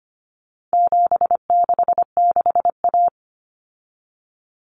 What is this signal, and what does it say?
Morse code, heard as a keyed tone, about 25 wpm, 700 Hz. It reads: 766A
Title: Common Cut Numbers - 4-Digit 25wpm